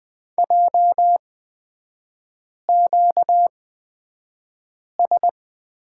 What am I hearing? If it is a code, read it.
Morse code, heard as a keyed tone, about 20 wpm, spaced as for 5 wpm, 700 Hz. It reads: JQS
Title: 236 Lesson 38e - Letter q - Three Characters 20wpm